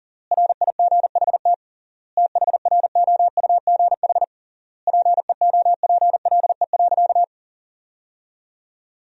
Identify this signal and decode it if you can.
Morse code, heard as a keyed tone, about 40 wpm, 700 Hz. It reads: RIGHT THROUGH PEOPLE.
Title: 3x Word-Spacing - Sentences from Top 200 Words - 40wpm